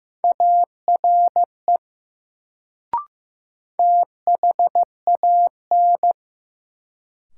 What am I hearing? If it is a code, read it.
Morse code, heard as a keyed tone, about 15 wpm, 700 Hz. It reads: ARE THAN
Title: Sentences from Top 100 Words - New Word Drill - 15wpm